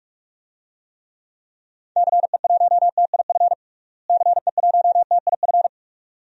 Morse code, heard as a keyed tone, about 43 wpm, 700 Hz. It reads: CE1TIF CE1TIF
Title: DXCC-100 Call Signs - 6 Characters - Repeated 2x - 45wpm